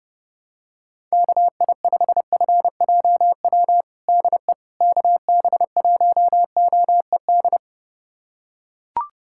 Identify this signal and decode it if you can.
Morse code, heard as a keyed tone, about 31 wpm, 700 Hz. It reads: KI5FJW DE KB1OED
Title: USA Callsign DE Callsign - 6 Characters - 30wpm